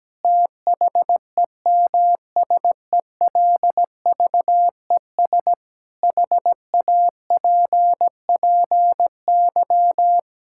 Morse code, heard as a keyed tone, about 17 wpm, 700 Hz. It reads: THEMSELVES HAPPY
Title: Sentences from Top 1600 Words 17wpm